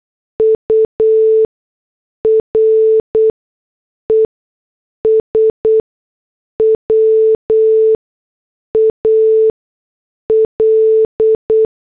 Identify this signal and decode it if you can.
Morse code, heard as a keyed tone, about 8 wpm, spaced as for 6 wpm, 440 Hz. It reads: URES WAL